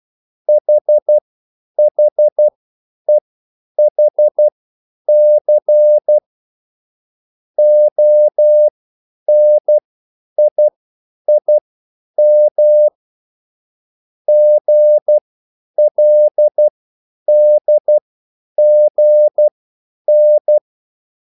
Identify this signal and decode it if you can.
Morse code, heard as a keyed tone, about 12 wpm, spaced as for 9 wpm, 600 Hz. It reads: HHEHC ONIIM GLDGN